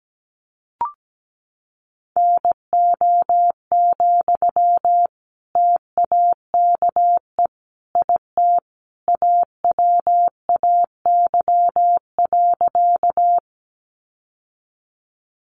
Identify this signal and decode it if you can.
Morse code, heard as a keyed tone, about 17 wpm, 700 Hz. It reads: NO, TAKE IT AWAY.
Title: Sentences from Top 600 Words 17wpm